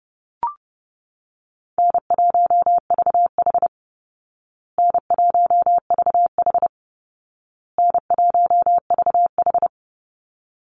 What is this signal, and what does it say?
Morse code, heard as a keyed tone, about 30 wpm, 700 Hz. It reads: N1VH N1VH N1VH
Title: Easy Call Signs Repeated 3x - 30wpm